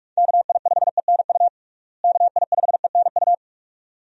Morse code, heard as a keyed tone, about 46 wpm, 700 Hz. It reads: KI5ENU KI5ENU
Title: USA Call Signs - 6 Characters - Repeated 2x - 45wpm